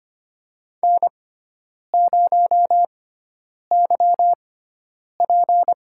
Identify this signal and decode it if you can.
Morse code, heard as a keyed tone, about 25 wpm, spaced as for 9 wpm, 700 Hz. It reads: N0YP
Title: USA Call Signs - 4 Characters - 6x Character Spacing - 25wpm